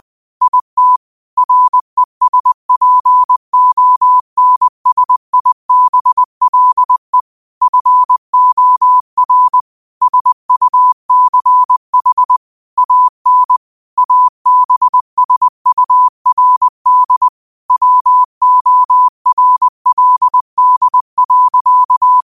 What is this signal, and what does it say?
Morse code, heard as a keyed tone, about 20 wpm, 1000 Hz. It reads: IT RESPONSIBLE FOR SUCH AN ABSURD WORLD.
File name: qotdfiles / 20wpm